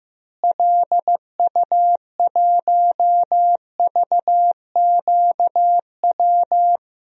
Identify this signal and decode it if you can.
Morse code, heard as a keyed tone, about 15 wpm, 700 Hz. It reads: LU1VQW
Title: DXCC-100 Call Signs - 6 Characters - Repeated 2x - 15wpm